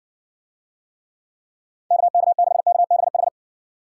Morse code, heard as a keyed tone, about 59 wpm, 700 Hz. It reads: KK6KBH